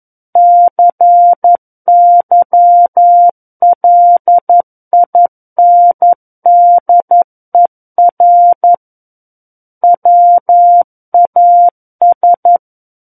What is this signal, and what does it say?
Morse code, heard as a keyed tone, about 11 wpm, 700 Hz. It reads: CYLINDER WAS
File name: Book One / 02-Chapter 2 - 11 WPM